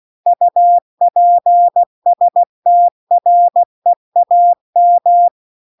Morse code, heard as a keyed tone, about 16 wpm, 700 Hz. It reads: UPSTREAM